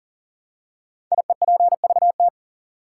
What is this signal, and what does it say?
Morse code, heard as a keyed tone, about 39 wpm, 700 Hz. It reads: IEPVT
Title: Code Groups - 5 Letters - 40wpm